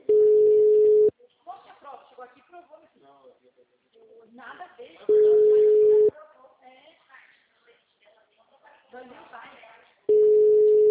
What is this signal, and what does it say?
Ringback tone (Europe, 425 Hz), with speech.